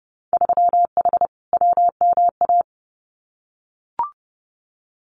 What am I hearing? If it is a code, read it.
Morse code, heard as a keyed tone, about 31 wpm, 700 Hz. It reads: 3H WMA
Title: Winter Field Day Exchanges Encore 30wpm